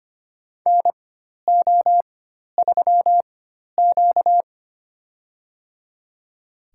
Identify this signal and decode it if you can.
Morse code, heard as a keyed tone, about 25 wpm, spaced as for 12 wpm, 700 Hz. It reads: NO3Q
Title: USA Call Signs - 4x Character Spacing - 25wpm